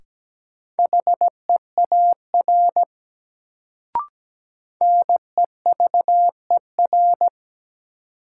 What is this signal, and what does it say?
Morse code, heard as a keyed tone, about 17 wpm, 700 Hz. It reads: HEAR NEVER
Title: Sentences from Top 300 Words - New Word Drill - 17wpm